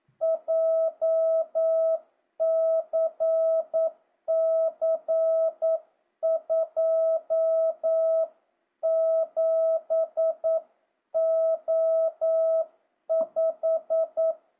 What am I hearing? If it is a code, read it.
Morse code, heard as a keyed tone, about 9 wpm, 650 Hz. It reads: JCC 27O5
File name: jh8cbh / 2025cw